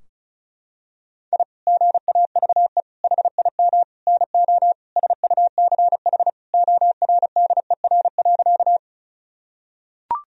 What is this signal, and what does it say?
Morse code, heard as a keyed tone, about 36 wpm, 700 Hz. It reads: I GAVE HIM NO SUCH ORDER.
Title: Sentences from Top 1000 Words - Review Part 2 - 35wpm